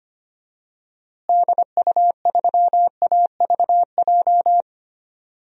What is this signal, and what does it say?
Morse code, heard as a keyed tone, about 25 wpm, 700 Hz. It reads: DU3AVJ
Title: DXCC-100 Call Signs - 6 Characters - Repeated 2x - 25wpm